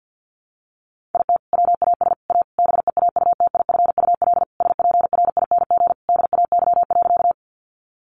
Morse code, heard as a keyed tone, about 50 wpm, 700 Hz. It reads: IT WAS A BEAUTIFUL SPRING DAY.
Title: Sentences from Top 1200 Words 50wpm